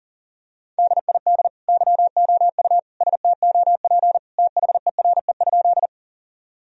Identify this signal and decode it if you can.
Morse code, heard as a keyed tone, about 40 wpm, 700 Hz. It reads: DID YOU STOP THERE?
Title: Sentences from Top 1000 Words - Review Part 1 - 40wpm